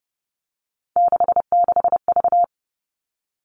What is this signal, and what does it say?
Morse code, heard as a keyed tone, about 30 wpm, 700 Hz. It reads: 66V
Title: Three Challenging Letter-Number Combinations 30wpm